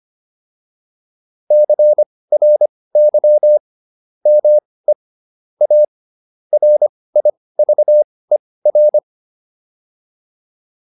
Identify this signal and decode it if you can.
Morse code, heard as a keyed tone, about 25 wpm, spaced as for 18 wpm, 600 Hz. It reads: CRY ME A RIVER